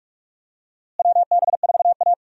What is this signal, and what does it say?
Morse code, heard as a keyed tone, about 44 wpm, 700 Hz. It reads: WB4A
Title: USA Call Signs - 4 Characters - Repeated 3x - 45wpm